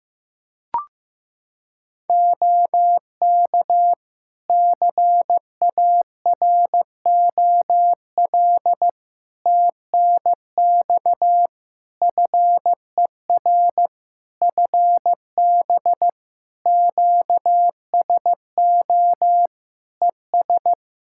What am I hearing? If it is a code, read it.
Morse code, heard as a keyed tone, about 14 wpm, 700 Hz. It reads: OK CAROL TNX FER FB QSO ES
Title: Practice QSO - Exchange 4 - Ending - 15wpm